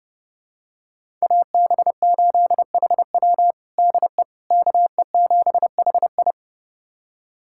Morse code, heard as a keyed tone, about 30 wpm, 700 Hz. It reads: AB8HW DE KE7HI
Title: USA Callsign DE Callsign - 5 Characters - 30wpm